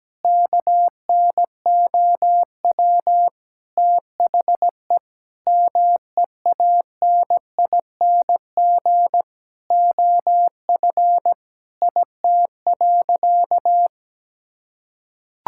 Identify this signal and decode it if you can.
Morse code, heard as a keyed tone, about 17 wpm, 700 Hz. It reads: KNOW THE MEANING OF IT.
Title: Sentences from Top 1000 Words - Review Part 6 - 17wpm